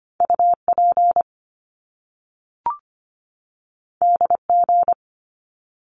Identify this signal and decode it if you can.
Morse code, heard as a keyed tone, about 25 wpm, 700 Hz. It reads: UP DG
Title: Top Two Letter Combinations in Words 25wpm